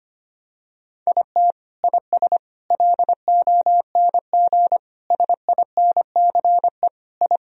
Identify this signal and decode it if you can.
Morse code, heard as a keyed tone, about 25 wpm, 700 Hz. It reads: IT IS LONG SINCE I
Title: Sentences from Top 1000 Words - Review Part 1 - 25wpm